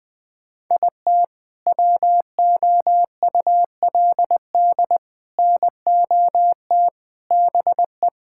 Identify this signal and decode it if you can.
Morse code, heard as a keyed tone, about 21 wpm, 700 Hz. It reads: IT WOULD NOT BE
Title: Sentences with 3-Word Phrases - 20wpm